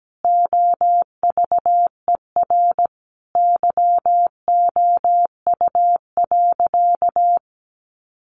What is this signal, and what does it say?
Morse code, heard as a keyed tone, about 17 wpm, 700 Hz. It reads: OVER YOU.
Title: Sentences from Top 900 Words 17wpm